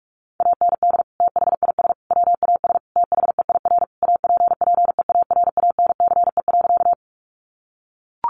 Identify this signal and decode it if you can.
Morse code, heard as a keyed tone, about 44 wpm, 700 Hz. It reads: AND THIS WAS THEIR APPEARANCE.
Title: Sentences from Top 1100 Words 45wpm